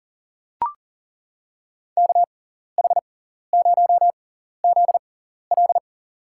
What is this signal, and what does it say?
Morse code, heard as a keyed tone, about 39 wpm, spaced as for 14 wpm, 700 Hz. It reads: KH0ZL
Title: USA Call Signs - 5 Characters - 6x Character Spacing - 40wpm